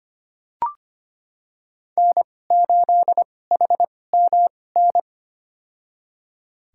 Morse code, heard as a keyed tone, about 25 wpm, spaced as for 18 wpm, 700 Hz. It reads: N8HMN
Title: USA Call Signs - 5 Characters - 2x Character Spacing - 25wpm